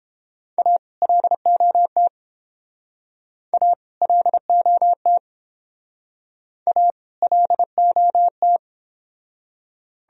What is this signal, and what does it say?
Morse code, heard as a keyed tone, about 28 wpm, 700 Hz. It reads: A LOT A LOT A LOT